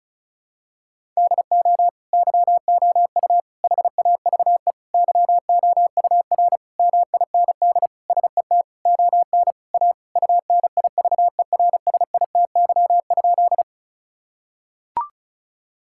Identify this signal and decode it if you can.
Morse code, heard as a keyed tone, about 34 wpm, 700 Hz. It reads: DO YOU HAVE YOUR MIND SET ON A UNIVERSITY?
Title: Sentences from Top 1000 Words - Review Part 5 - 35wpm